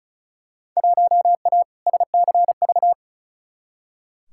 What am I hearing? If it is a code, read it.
Morse code, heard as a keyed tone, about 36 wpm, 700 Hz. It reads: 1A SCV
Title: Field Day Exchanges - 35wpm